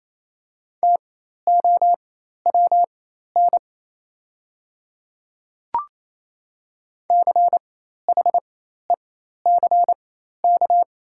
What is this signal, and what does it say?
Morse code, heard as a keyed tone, about 28 wpm, spaced as for 13 wpm, 690 Hz. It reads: TOWN CHECK
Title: 4x Character Spacing - Top 500 Words - 28wpm